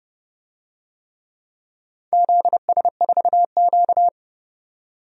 Morse code, heard as a keyed tone, about 28 wpm, 700 Hz. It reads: ZS4Q
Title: DXCC-100 Call Signs - 4 Characters - 30wpm